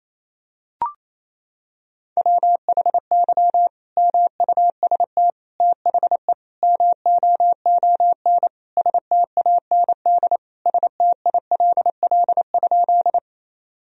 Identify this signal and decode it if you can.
Morse code, heard as a keyed tone, about 28 wpm, 700 Hz. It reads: WHY MUST THE MOON STAND STILL?
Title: Sentences from Top 1000 Words - Review Part 3 - 28wpm